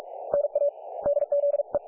Morse code, heard as a keyed tone, about 44 wpm, spaced as for 31 wpm, 570 Hz. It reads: NADZE
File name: dlazaawansowanych / nagraniazpasma / websdr recording 3564.7kHz